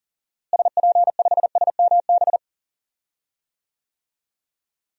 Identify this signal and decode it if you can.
Morse code, heard as a keyed tone, about 37 wpm, 700 Hz. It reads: SP5SMB
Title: DXCC-100 Call Signs - 6 Characters - Repeated 2x - 40wpm